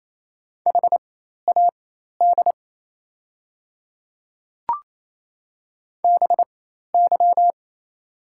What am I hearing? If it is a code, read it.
Morse code, heard as a keyed tone, about 27 wpm, spaced as for 13 wpm, 690 Hz. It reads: HAD BY
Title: 4x Character Spacing - Top 100 Words - 28wpm